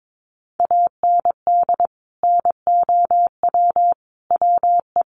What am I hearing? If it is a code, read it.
Morse code, heard as a keyed tone, about 22 wpm, 700 Hz. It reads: AND NOW WE
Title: Sentences from Top 1500 Words 22wpm